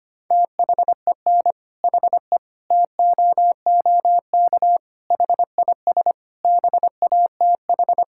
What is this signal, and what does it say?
Morse code, heard as a keyed tone, about 25 wpm, 700 Hz. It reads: THEN HE TOOK HIS BATH
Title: Sentences from Top 1600 Words 25wpm